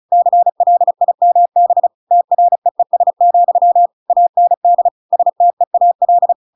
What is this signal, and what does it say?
Morse code, heard as a keyed tone, about 34 wpm, 700 Hz. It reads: CLIMB TREES, AND STEAL